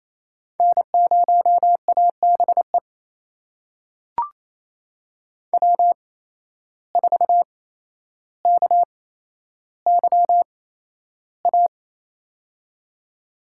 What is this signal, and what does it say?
Morse code, heard as a keyed tone, about 28 wpm, spaced as for 12 wpm, 700 Hz. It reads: N0ABE W4KYA
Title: USA Call Signs - 5 Characters - 8x Character Spacing - 28wpm